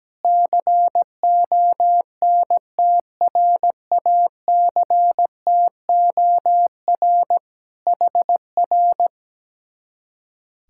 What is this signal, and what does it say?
Morse code, heard as a keyed tone, about 17 wpm, 700 Hz. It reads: CONTRACTOR HR
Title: Speed-Racing Common QSO Phrases - 17wpm